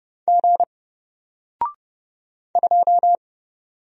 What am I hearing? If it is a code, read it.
Morse code, heard as a keyed tone, about 30 wpm, 700 Hz. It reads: G 2
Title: Single Letter-Number Alt 30wpm